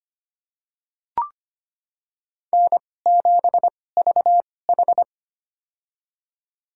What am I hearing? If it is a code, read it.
Morse code, heard as a keyed tone, about 25 wpm, spaced as for 18 wpm, 700 Hz. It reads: N7VH
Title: USA Call Signs - 4 Characters - 2x Character Spacing - 25wpm